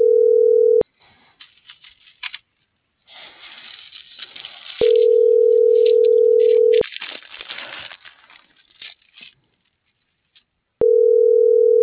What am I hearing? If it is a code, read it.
Ringback tone (North America), 440 + 480 Hz.